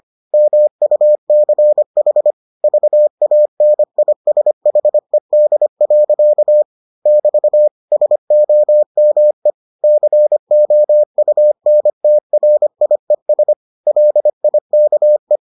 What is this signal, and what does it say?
Morse code, heard as a keyed tone, about 25 wpm, 600 Hz. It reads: MUCH VANISHED. <BT> SOME COUNTRIES LIKE